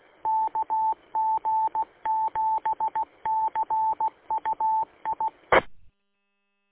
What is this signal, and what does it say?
Morse code, heard as a keyed tone, about 16 wpm, 880 Hz. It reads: KG7CUI E
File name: w8cmn-vhf-combined